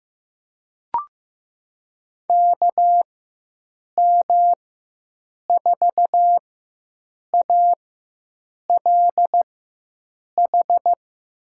Morse code, heard as a keyed tone, about 15 wpm, spaced as for 7 wpm, 700 Hz. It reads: KM4ALH
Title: USA Call Signs - 4x Character Spacing - 15wpm